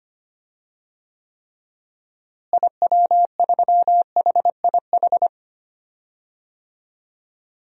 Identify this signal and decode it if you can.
Morse code, heard as a keyed tone, about 25 wpm, 700 Hz. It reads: IW3HIH
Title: DXCC-100 Call Signs - 6 Characters - Repeated 2x - 25wpm